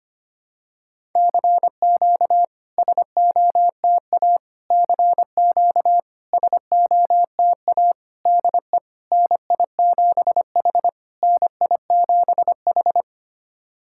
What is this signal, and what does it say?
Morse code, heard as a keyed tone, about 25 wpm, 700 Hz. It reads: CQ SOTA CQ SOTA DE NI7H NI7H
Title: SOTA QSO Practice 25wpm